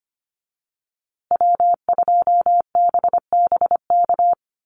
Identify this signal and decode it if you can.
Morse code, heard as a keyed tone, about 25 wpm, 700 Hz. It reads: W2BBK
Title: USA Call Signs Repeated 2x 25wpm